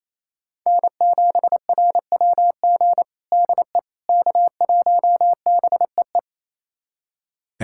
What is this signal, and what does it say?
Morse code, heard as a keyed tone, about 28 wpm, 700 Hz. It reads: N7RWG DE K1BEE
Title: USA Callsign DE Callsign - 5 Characters - 28wpm